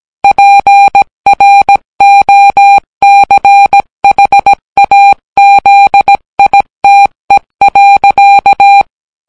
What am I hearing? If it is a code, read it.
Morse code, heard as a keyed tone, about 17 wpm, 800 Hz. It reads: PROCHAZITE.